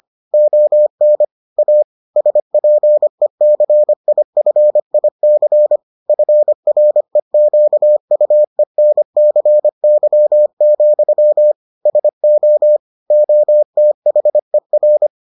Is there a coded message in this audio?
Yes: Morse code from 0.3 to 15.1 s, heard as a keyed tone, about 25 wpm, 600 Hz. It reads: ON A SPECIFIC FREQUENCY, SO OTHER